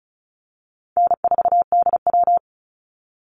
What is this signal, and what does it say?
Morse code, heard as a keyed tone, about 34 wpm, 700 Hz. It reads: N4DW
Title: Call Signs - 35wpm